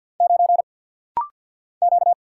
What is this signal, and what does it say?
Morse code, heard as a keyed tone, about 48 wpm, 700 Hz. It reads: 9 Q
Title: Single Letter-Number - Rapid-Fire 50wpm